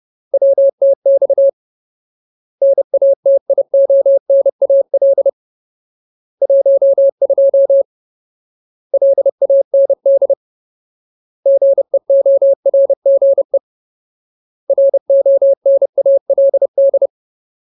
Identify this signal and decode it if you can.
Morse code, heard as a keyed tone, about 30 wpm, 550 Hz. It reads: WTX NATIONAL 12 LAND GEORGE RONALD
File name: Ultimate Word List / 30WPM Ultimate